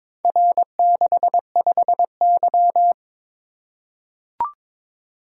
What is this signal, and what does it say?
Morse code, heard as a keyed tone, about 23 wpm, 700 Hz. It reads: R65Y
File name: practice / 22wpm / Four Letter-Number Combinations - 22wpm